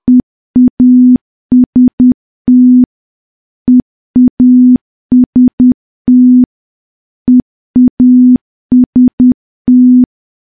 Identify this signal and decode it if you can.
Morse code, heard as a keyed tone, about 10 wpm, 250 Hz. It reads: EAST EAST EAST
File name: morse / morse-training-10WPM